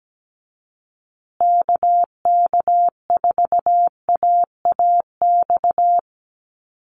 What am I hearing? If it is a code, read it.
Morse code, heard as a keyed tone, about 17 wpm, 700 Hz. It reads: KK4AAX
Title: USA Call Signs Repeated 3x 17wpm